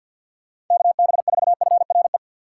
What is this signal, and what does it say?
Morse code, heard as a keyed tone, about 52 wpm, 700 Hz. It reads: KB4FRE
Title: USA Call Signs - 6x Character Spacing - 50wpm